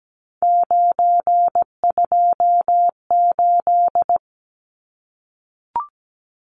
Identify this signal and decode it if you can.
Morse code, heard as a keyed tone, about 17 wpm, 700 Hz. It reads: 928
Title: Numbers - 3-Digit 17wpm